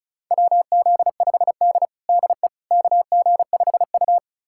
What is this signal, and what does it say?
Morse code, heard as a keyed tone, about 36 wpm, 700 Hz. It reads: WZ5D DE KG5U
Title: USA Callsign DE Callsign - 4 Characters - 35wpm